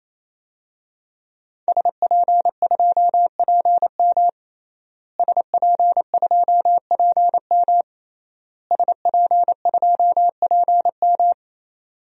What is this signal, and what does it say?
Morse code, heard as a keyed tone, about 28 wpm, 700 Hz. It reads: SP2PM SP2PM SP2PM
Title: DXCC-100 Call Signs - 5 Characters - Repeated 3x - 28wpm